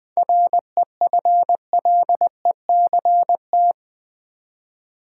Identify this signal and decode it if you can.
Morse code, heard as a keyed tone, about 20 wpm, 700 Hz. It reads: REFLECT
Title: Seven Letter Words - 20wpm